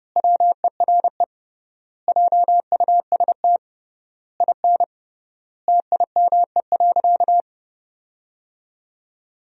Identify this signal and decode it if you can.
Morse code, heard as a keyed tone, about 29 wpm, 700 Hz. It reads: WERE JUST IN TIME.
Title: 3x Word-Spacing - Sentences from Top 300 Words - 30wpm